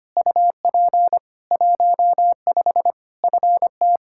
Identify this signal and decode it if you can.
Morse code, heard as a keyed tone, about 25 wpm, 700 Hz. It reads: UP 15 FT
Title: Practice QSO - Exchange 2 - Info - 25wpm